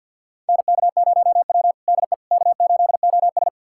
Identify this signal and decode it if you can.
Morse code, heard as a keyed tone, about 51 wpm, 700 Hz. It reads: NK0W DE K8OS
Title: USA Callsign DE Callsign - 4 Characters - 50wpm